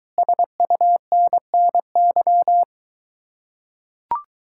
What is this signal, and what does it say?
Morse code, heard as a keyed tone, about 23 wpm, 700 Hz. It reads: SUNNY